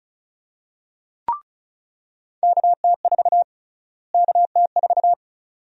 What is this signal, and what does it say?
Morse code, heard as a keyed tone, about 36 wpm, 700 Hz. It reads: KT4 KT4
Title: USA Call Sign Prefixes - Repeated 2x - 35wpm